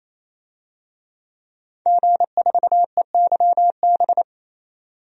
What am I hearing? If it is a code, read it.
Morse code, heard as a keyed tone, about 28 wpm, 700 Hz. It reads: G4EYB
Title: DXCC-100 Call Signs - 5 Characters - 28wpm